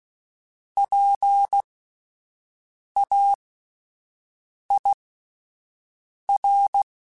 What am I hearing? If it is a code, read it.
Morse code, heard as a keyed tone, about 16 wpm, spaced as for 5 wpm, 800 Hz. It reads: PAIR